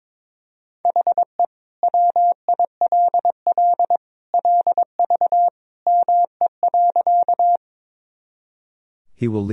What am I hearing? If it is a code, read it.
Morse code, heard as a keyed tone, about 22 wpm, 700 Hz. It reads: HE WILL LV ME.
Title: Sentences from Top 400 Words and Abbreviations 22wpm